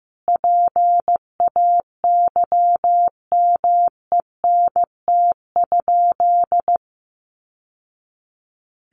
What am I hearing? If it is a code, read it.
Morse code, heard as a keyed tone, about 15 wpm, 700 Hz. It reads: PAYMENT?
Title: Sentences from Top 1200 Words 15wpm